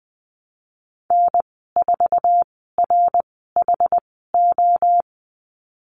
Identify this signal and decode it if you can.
Morse code, heard as a keyed tone, about 20 wpm, spaced as for 15 wpm, 700 Hz. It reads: N4RHO
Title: 062 Lesson 11g - Letter h - Call Signs 20wpm